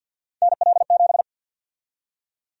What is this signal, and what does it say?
Morse code, heard as a keyed tone, about 51 wpm, 700 Hz. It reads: NL7
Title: USA Call Sign Prefixes - 50wpm